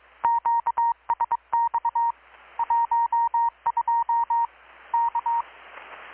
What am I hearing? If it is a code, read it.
Morse code, heard as a keyed tone, about 21 wpm, 930 Hz. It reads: QSX 12 K
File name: blog / swl / 6-abril-2019-12695-USB-KPS